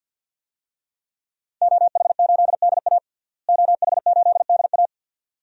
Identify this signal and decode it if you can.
Morse code, heard as a keyed tone, about 50 wpm, 700 Hz. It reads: OH8DA OH8DA
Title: DXCC-100 Call Signs - 5 Characters - Repeated 2x - 50wpm